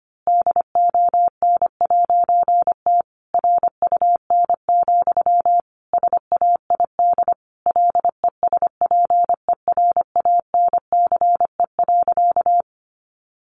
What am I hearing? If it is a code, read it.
Morse code, heard as a keyed tone, about 25 wpm, 700 Hz. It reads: DON'T RUN, SAID LESPERANCE.
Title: A Sound Of Thunder 25wpm